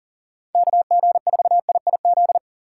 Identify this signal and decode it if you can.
Morse code, heard as a keyed tone, about 40 wpm, 700 Hz. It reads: KG4IIZ
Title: USA Call Signs - 6 Characters - 4x Character Spacing - 40wpm